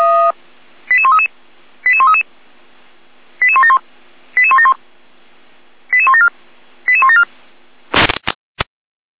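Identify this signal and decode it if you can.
Five-tone selective call (ZVEI), heard as quick runs of five tones, tones of about 70 ms. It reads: 88133 88133 88171 88171 88174 88174
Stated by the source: ZVEI five-tone selective call 88241 received on SDR.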